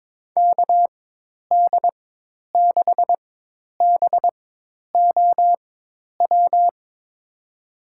Morse code, heard as a keyed tone, about 22 wpm, spaced as for 10 wpm, 700 Hz. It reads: KD6BOW
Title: USA Call Signs - 6 Characters - 4x Character Spacing - 22wpm